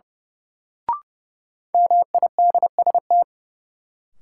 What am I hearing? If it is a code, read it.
Morse code, heard as a keyed tone, about 30 wpm, 700 Hz. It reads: MIDST
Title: Sets of 4 Words - New Word Drill - 30wpm